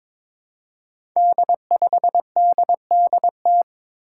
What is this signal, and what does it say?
Morse code, heard as a keyed tone, about 22 wpm, 700 Hz. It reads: D5DDT
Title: Five Challenging Letter-Number Combinations 22wpm